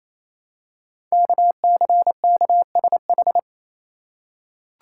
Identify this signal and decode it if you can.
Morse code, heard as a keyed tone, about 27 wpm, 700 Hz. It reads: KCKSH
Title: Code Groups - 5 Letters - 28wpm